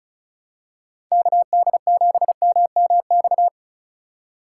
Morse code, heard as a keyed tone, about 36 wpm, 700 Hz. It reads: KD7MMX
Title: USA Call Signs - 6 Characters - 3x Character Spacing - 35wpm